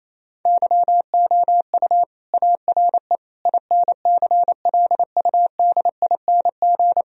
Morse code, heard as a keyed tone, about 28 wpm, 700 Hz. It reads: YOU ARE INCLUDING
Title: Sentences from Top 1000 Words - Review Part 4 - 28wpm